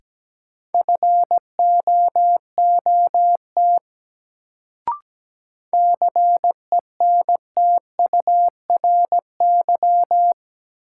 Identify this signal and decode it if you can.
Morse code, heard as a keyed tone, about 17 wpm, 700 Hz. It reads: FOOT CENTURY
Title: Sets of 5 Words - New Word Drill - 17wpm